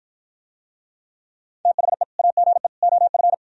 Morse code, heard as a keyed tone, about 52 wpm, 700 Hz. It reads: THE AGE OF